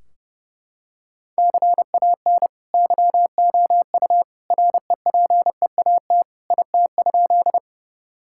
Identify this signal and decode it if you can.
Morse code, heard as a keyed tone, about 32 wpm, 700 Hz. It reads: CAN YOU REPEAT IT?
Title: Sentences from Top 1600 Words 30wpm